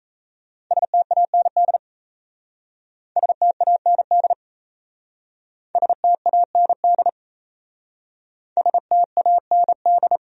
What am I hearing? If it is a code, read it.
Morse code, heard as a keyed tone, about 36 wpm, 700 Hz. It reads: STAND STAND STAND STAND